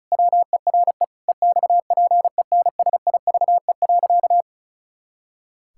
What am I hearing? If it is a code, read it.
Morse code, heard as a keyed tone, about 34 wpm, 700 Hz. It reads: WERE EXPENSIVE.
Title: Pangram Challenge v2 - 35wpm